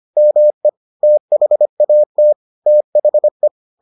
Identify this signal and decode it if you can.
Morse code, heard as a keyed tone, about 25 wpm, 600 Hz. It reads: ME THAT THE